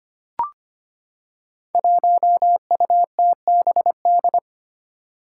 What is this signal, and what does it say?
Morse code, heard as a keyed tone, about 25 wpm, 700 Hz. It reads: 1UTBD
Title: Five Challenging Letter-Number Combinations 25wpm